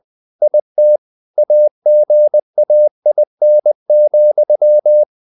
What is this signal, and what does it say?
Morse code, heard as a keyed tone, about 20 wpm, 600 Hz. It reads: IT AGAIN,
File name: the old man and the sea part 2 20wpm